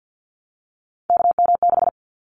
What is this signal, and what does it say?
Morse code, heard as a keyed tone, about 50 wpm, 700 Hz. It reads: KM6
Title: Callsign Prefixes for Top 25 DXCC - Repeated 3x - 50wpm